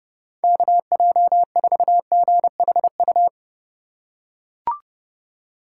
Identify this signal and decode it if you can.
Morse code, heard as a keyed tone, about 28 wpm, 700 Hz. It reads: KJ4GHU